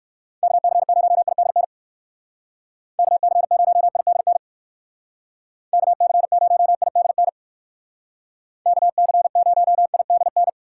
Morse code, heard as a keyed tone, about 59 wpm, 700 Hz. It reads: KC0IDN KC0IDN KC0IDN KC0IDN